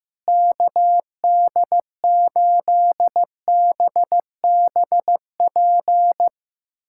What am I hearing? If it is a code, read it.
Morse code, heard as a keyed tone, about 15 wpm, 700 Hz. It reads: KD8BBP
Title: USA Callsign DE Callsign 15wpm